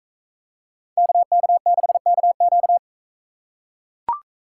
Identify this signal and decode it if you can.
Morse code, heard as a keyed tone, about 40 wpm, 700 Hz. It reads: KK6KQ